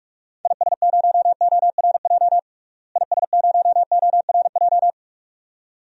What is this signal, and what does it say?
Morse code, heard as a keyed tone, about 46 wpm, 700 Hz. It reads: IS0ORJ IS0ORJ
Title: DXCC-100 Call Signs - 6 Characters - Repeated 2x - 45wpm